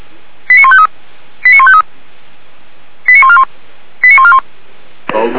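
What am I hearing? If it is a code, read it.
Five-tone selective call (ZVEI), heard as quick runs of five tones, tones of about 70 ms. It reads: 89153 89153 89141 89141
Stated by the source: ZVEI five-tone call 89271 received by SDR.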